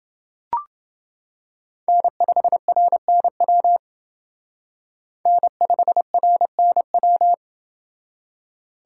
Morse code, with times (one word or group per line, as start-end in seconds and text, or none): 1.9-3.8 N5RNW
5.3-7.3 N5RNW